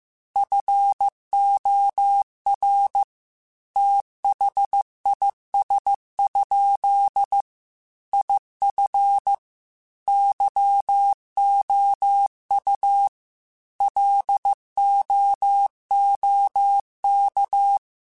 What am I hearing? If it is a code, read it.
Morse code, heard as a keyed tone, about 15 wpm, 800 Hz. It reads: FOR THIS? IF YOU LOOK
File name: Technical Subjects / Technical Subjects-mp3s / OR Gate @13